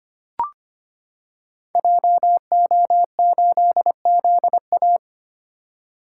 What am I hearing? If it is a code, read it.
Morse code, heard as a keyed tone, about 25 wpm, 700 Hz. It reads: JO8ZA
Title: Code Groups - 5 Letter:Numbers - 25wpm